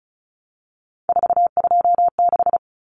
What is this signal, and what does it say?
Morse code, heard as a keyed tone, about 34 wpm, 700 Hz. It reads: 426
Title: Common Cut Numbers - 3-Digit 35wpm